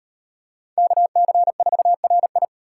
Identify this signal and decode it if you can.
Morse code, heard as a keyed tone, about 36 wpm, 700 Hz. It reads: KC4RI